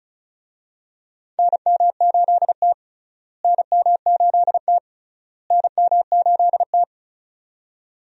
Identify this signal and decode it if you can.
Morse code, heard as a keyed tone, about 34 wpm, 700 Hz. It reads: NM8T NM8T NM8T
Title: USA Call Signs - 4 Characters - Repeated 3x - 35wpm